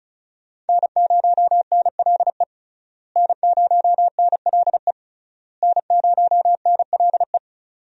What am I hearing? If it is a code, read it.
Morse code, heard as a keyed tone, about 34 wpm, 700 Hz. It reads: N0NLE N0NLE N0NLE
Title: USA Call Signs - 5 Characters - Repeated 3x - 35wpm